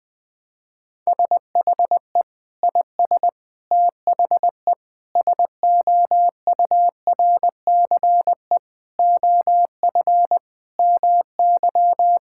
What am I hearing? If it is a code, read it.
Morse code, heard as a keyed tone, about 20 wpm, 700 Hz. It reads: SHE IS THE SOURCE OF MY
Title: Sentences from Top 1000 Words - Review Part 5 - 20wpm